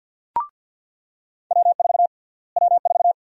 Morse code, heard as a keyed tone, about 49 wpm, 700 Hz. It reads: W4 W4
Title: USA Call Sign Prefixes - Repeated 2x - 50wpm